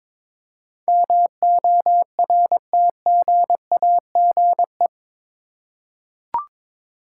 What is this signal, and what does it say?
Morse code, heard as a keyed tone, about 22 wpm, 700 Hz. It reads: MORTGAGE
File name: practice / 22wpm / Eight Letter Words - 22wpm